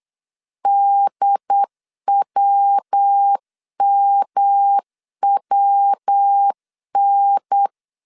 Morse code, heard as a keyed tone, about 8 wpm, 800 Hz. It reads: DWMWN
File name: CW / DET / 1 3 1500C S C 8wpm 4214